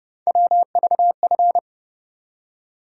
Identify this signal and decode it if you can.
Morse code, heard as a keyed tone, about 32 wpm, 700 Hz. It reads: WVF
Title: USA Call Sign Postfixes - 3 Letters 30wpm